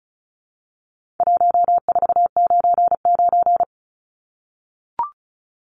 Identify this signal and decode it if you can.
Morse code, heard as a keyed tone, about 34 wpm, 700 Hz. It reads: 1499
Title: Numbers - 4-Digit 35wpm